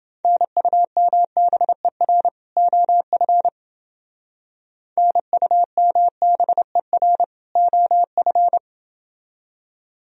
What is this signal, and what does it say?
Morse code, heard as a keyed tone, about 29 wpm, 700 Hz. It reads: NUMBER OF NUMBER OF